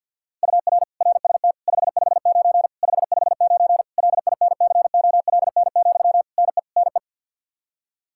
Morse code, heard as a keyed tone, about 53 wpm, 700 Hz. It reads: UR RST 559 559 LINCOLN, NE NE
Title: Practice QSO - Exchange 1 - Beginning - 50wpm